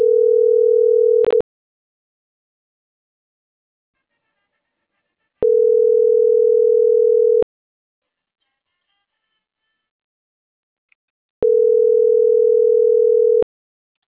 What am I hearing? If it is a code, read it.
Ringback tone (North America), 440 + 480 Hz.